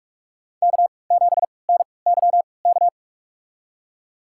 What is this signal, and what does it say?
Morse code, heard as a keyed tone, about 46 wpm, spaced as for 26 wpm, 700 Hz. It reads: K7NYK
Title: USA Call Signs - 3x Character Spacing - 45wpm